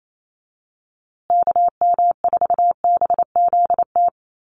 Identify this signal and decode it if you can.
Morse code, heard as a keyed tone, about 28 wpm, 700 Hz. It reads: KM4BZT
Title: USA Call Signs Repeated 2x 28wpm